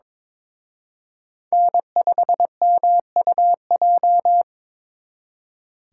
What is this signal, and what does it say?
Morse code, heard as a keyed tone, about 26 wpm, 700 Hz. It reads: N5MUJ
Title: Speed Racing - USA 1x3 Call Signs 22wpm